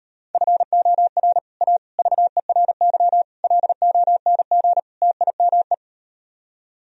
Morse code, heard as a keyed tone, about 38 wpm, 700 Hz. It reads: FOR A VERY LONG TIME